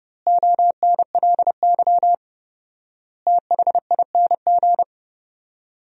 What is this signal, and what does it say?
Morse code, heard as a keyed tone, about 32 wpm, 700 Hz. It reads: ONLY THING
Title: 4x Word-Spacing - Sentences from Top 200 Words - 30wpm